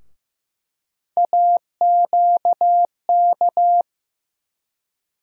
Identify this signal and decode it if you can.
Morse code, heard as a keyed tone, about 15 wpm, 700 Hz. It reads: AQK
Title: USA Call Sign Postfixes - 3 Letters 15wpm